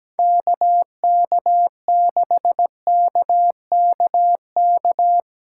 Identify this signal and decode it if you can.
Morse code, heard as a keyed tone, about 19 wpm, 700 Hz. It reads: KK6KKK
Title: Speed Racing - USA 2x3 Call Signs 17wpm